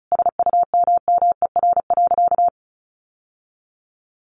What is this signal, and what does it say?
Morse code, heard as a keyed tone, about 34 wpm, 700 Hz. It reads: SUMMER.
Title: Sentences from Top 1100 Words 35wpm